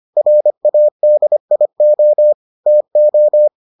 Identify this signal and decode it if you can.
Morse code, heard as a keyed tone, about 25 wpm, 600 Hz. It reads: RADIO TO